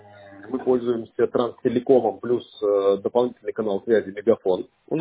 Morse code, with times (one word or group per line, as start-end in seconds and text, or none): none